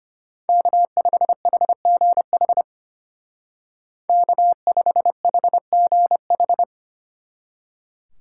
Morse code, heard as a keyed tone, about 33 wpm, 700 Hz. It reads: K5HGH K5HGH